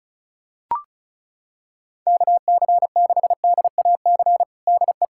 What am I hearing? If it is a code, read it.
Morse code, heard as a keyed tone, about 34 wpm, 700 Hz. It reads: KC6DAC DE
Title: USA Callsign DE Callsign - 6 Characters - 35wpm